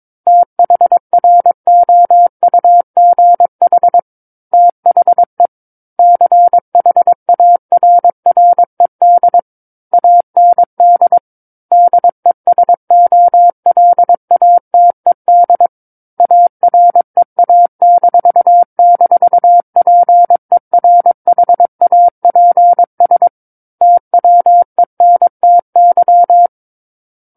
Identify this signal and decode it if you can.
Morse code, heard as a keyed tone, about 22 wpm, 700 Hz. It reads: THROUGH THE CHARRED AND DESOLATED AREA--PERHAPS TWENTY
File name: Book One / 13-Chapter 13 - 22 WPM